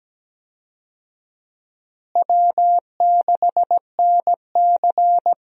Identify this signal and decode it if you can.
Morse code, heard as a keyed tone, about 17 wpm, 700 Hz. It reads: W6NC
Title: USA Call Signs - 4 Characters - Repeated 3x - 17wpm